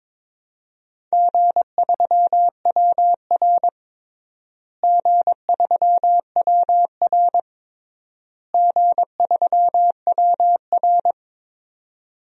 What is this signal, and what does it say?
Morse code, heard as a keyed tone, about 22 wpm, 700 Hz. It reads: G3WR G3WR G3WR
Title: DXCC-100 Call Signs - 4 Characters - Repeated 3x - 22wpm